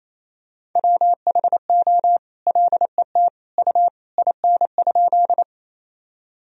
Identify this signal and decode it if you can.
Morse code, heard as a keyed tone, about 28 wpm, 700 Hz. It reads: WHO LET U IN?
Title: Sentences from Top 300 Words and Abbreviations 28wpm